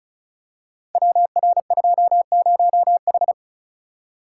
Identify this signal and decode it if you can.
Morse code, heard as a keyed tone, about 36 wpm, 700 Hz. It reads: WR20H
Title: Code Groups - 5 Letter:Numbers - 35wpm